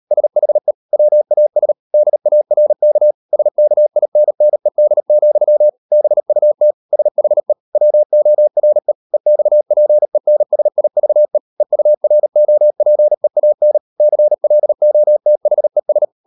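Morse code, heard as a keyed tone, about 38 wpm, 600 Hz. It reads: SHE WAS DARK SKINNED, BUT SHE WORE EXPENSIVE EUROPEAN CLOTHES
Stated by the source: Morse code practice at 38 WPM, sending Ken Follett's novel Never in English, from CW med Gnister.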